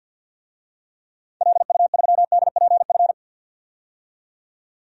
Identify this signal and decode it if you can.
Morse code, heard as a keyed tone, about 50 wpm, 700 Hz. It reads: LU3DPF
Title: DXCC-100 Call Signs - 6 Characters - 50wpm